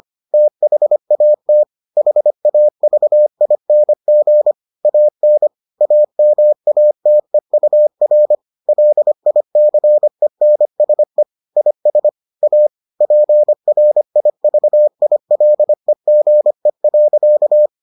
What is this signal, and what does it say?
Morse code, heard as a keyed tone, about 25 wpm, 600 Hz. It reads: THAT HAVING AN AMATEUR LICENSE IS A PRIVILEGE.